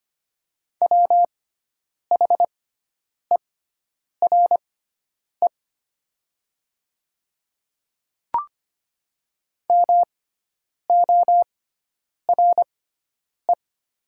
Morse code, heard as a keyed tone, about 25 wpm, spaced as for 9 wpm, 700 Hz. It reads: WHERE MORE
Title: 6x Character Spacing - Top 300 Words - 25wpm